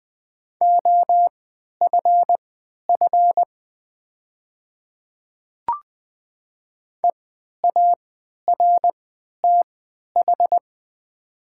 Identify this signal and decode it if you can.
Morse code, heard as a keyed tone, about 20 wpm, spaced as for 11 wpm, 700 Hz. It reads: OFF EARTH
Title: 3x Character Spacing - Top 500 Words - 20wpm